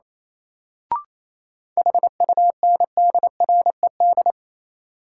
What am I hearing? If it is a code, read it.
Morse code, heard as a keyed tone, about 28 wpm, 700 Hz. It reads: HUNDRED
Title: Sentences from Top 400 Words - New Word Drill - 28wpm